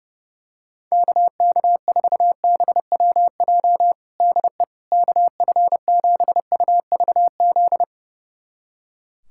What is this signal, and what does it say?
Morse code, heard as a keyed tone, about 30 wpm, 700 Hz. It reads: KK4BWJ DE KF7UVZ
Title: USA Callsign DE Callsign - 6 Characters - 30wpm